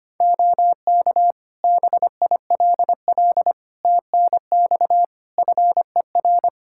Morse code, heard as a keyed tone, about 25 wpm, 700 Hz. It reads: OK BILL TNX FER
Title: Practice QSO - Exchange 4 - Ending - 25wpm